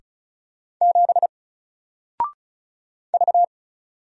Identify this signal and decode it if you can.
Morse code, heard as a keyed tone, about 36 wpm, 700 Hz. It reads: 7 V
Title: Single Letter-Number Alt 35wpm